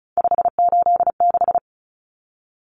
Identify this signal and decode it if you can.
Morse code, heard as a keyed tone, about 34 wpm, 700 Hz. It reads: 586
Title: Numbers - 3-Digit 35wpm